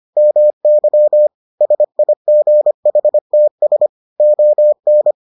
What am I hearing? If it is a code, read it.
Morse code, heard as a keyed tone, about 25 wpm, 600 Hz. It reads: MY SIGHTS ON